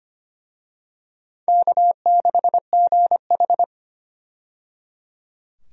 Morse code, heard as a keyed tone, about 25 wpm, 700 Hz. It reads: K6GH
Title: USA Call Signs - 4 Characters - 25wpm